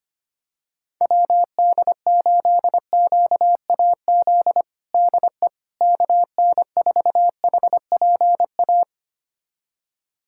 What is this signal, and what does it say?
Morse code, heard as a keyed tone, about 25 wpm, 700 Hz. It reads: WD8QAZ DE KN4HPA
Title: USA Callsign DE Callsign - 6 Characters - 25wpm